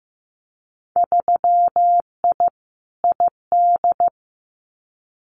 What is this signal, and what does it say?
Morse code, heard as a keyed tone, about 15 wpm, 700 Hz. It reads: 3I ID
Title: Winter Field Day Exchanges 15wpm